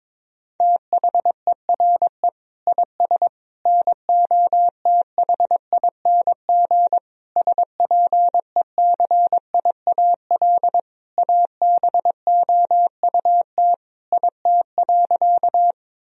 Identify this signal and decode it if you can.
Morse code, heard as a keyed tone, about 22 wpm, 700 Hz. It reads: THERE IS NOTHING SPECIAL ABOUT IT.
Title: Sentences from Top 1000 Words - Review Part 3 - 22wpm